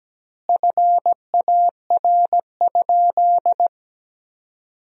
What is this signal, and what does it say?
Morse code, heard as a keyed tone, about 17 wpm, 700 Hz. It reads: FAR?
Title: 2x Word-Spacing - Sentences from Top 300 Words - 17wpm